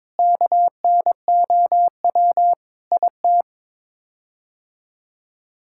Morse code, heard as a keyed tone, about 21 wpm, 700 Hz. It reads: KNOW IT
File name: Sets of 9 Words - 22wpm